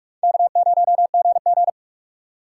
Morse code, heard as a keyed tone, about 48 wpm, 700 Hz. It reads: K0GG
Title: USA Call Signs - 4 Characters - 8x Character Spacing - 45wpm